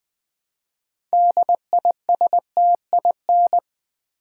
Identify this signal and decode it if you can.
Morse code, heard as a keyed tone, about 20 wpm, 700 Hz. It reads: DISTIN
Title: Build A Word - 2 Letters At A Time - 8-Letter Words - 20wpm